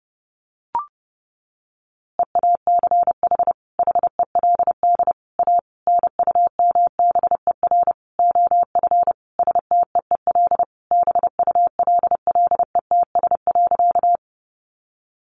Morse code, heard as a keyed tone, about 30 wpm, 700 Hz. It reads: EACH HELD A NUMBER OF STEEL BULLETS.
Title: A Sound Of Thunder 30wpm